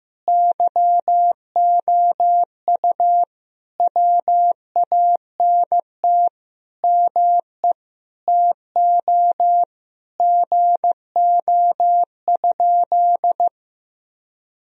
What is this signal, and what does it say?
Morse code, heard as a keyed tone, about 16 wpm, 700 Hz. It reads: YOU WANT ME TO GO?
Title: Sentences from Top 1000 Words - Review Part 3 - 15wpm